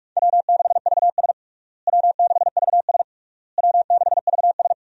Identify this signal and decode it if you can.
Morse code, heard as a keyed tone, about 46 wpm, 700 Hz. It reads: W6VS W6VS W6VS
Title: USA Call Signs - 4 Characters - Repeated 3x - 45wpm